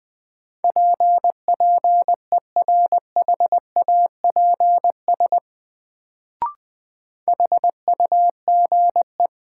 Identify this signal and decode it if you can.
Morse code, heard as a keyed tone, about 21 wpm, 700 Hz. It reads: PPERHAPS HUGE
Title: Sentences from Top 1100 Words - New Word Drill Alt - 20wpm